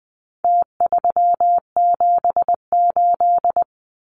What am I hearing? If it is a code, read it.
Morse code, heard as a keyed tone, about 20 wpm, 700 Hz. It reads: T378
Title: Numbers - Contest Style - 4-Digit 20wpm